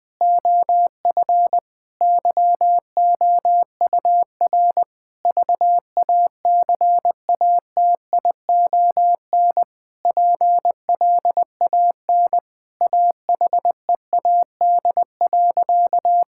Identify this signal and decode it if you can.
Morse code, heard as a keyed tone, about 21 wpm, 700 Hz. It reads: OF YOUR VACATION PLAN AHEAD.
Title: Sentences with 7-Word Phrases - 20wpm